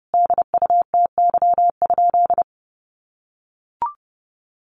Morse code, heard as a keyed tone, about 30 wpm, 700 Hz. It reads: DUTY?
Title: Sentences from Top 1500 Words 30wpm